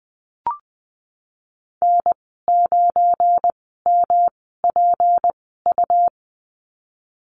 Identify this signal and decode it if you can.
Morse code, heard as a keyed tone, about 20 wpm, spaced as for 15 wpm, 700 Hz. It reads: N9MPU
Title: 173 Lesson 28g - Number 9 - Call Signs 20wpm